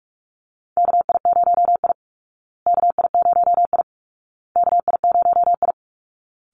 Morse code, heard as a keyed tone, about 44 wpm, 700 Hz. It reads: KI0I KI0I KI0I
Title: Call Signs Repeated 2x - 45wpm